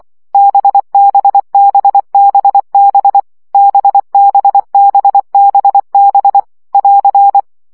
Morse code, heard as a keyed tone, about 24 wpm, 800 Hz. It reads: BBBBB BBBBB <AR>
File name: CW gyakorlas b 20 baud